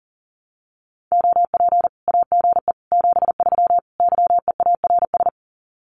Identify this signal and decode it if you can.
Morse code, heard as a keyed tone, about 40 wpm, 700 Hz. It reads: OP AGE 73 YEARS
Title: Common QSO Phrases 40wpm